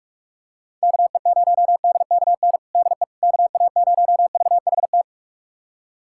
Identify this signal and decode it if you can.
Morse code, heard as a keyed tone, about 41 wpm, 700 Hz. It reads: KE0DKN DE KA0VHT
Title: USA Callsign DE Callsign - 6 Characters - 45wpm